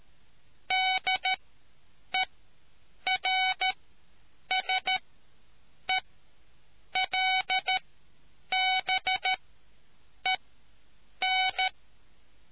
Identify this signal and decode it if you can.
Morse code, heard as a keyed tone, about 13 wpm, spaced as for 7 wpm, 750 Hz. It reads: DERSELBEN